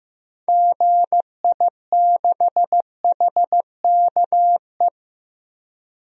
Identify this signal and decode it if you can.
Morse code, heard as a keyed tone, about 15 wpm, 700 Hz. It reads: GI6HKE
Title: DXCC-100 Call Signs - 6 Characters - 15wpm